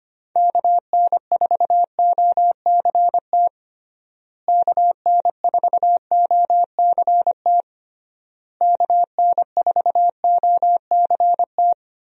Morse code, heard as a keyed tone, about 25 wpm, 700 Hz. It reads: KN4OCT KN4OCT KN4OCT
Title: USA Call Signs - 6 Characters - Repeated 3x - 25wpm